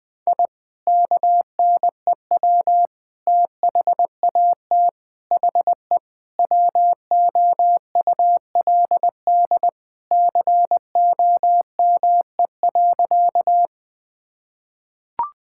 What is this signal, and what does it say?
Morse code, heard as a keyed tone, about 19 wpm, 700 Hz. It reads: I KNEW THAT HE WOULD COME.
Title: Sentences with 2-Word Phrases - Spoken Beginning - 20wpm